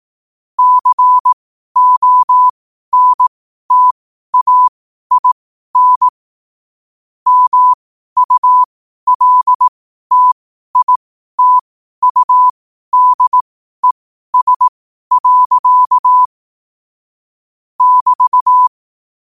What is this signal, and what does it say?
Morse code, heard as a keyed tone, about 18 wpm, spaced as for 12 wpm, 1000 Hz. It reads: CONTAIN MULTITUDES. <BT>
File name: qotdfiles / 12wpm